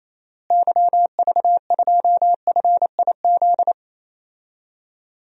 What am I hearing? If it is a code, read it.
Morse code, heard as a keyed tone, about 28 wpm, 700 Hz. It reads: YV2FIZ
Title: DXCC-100 Call Signs - 6 Characters - Repeated 2x - 28wpm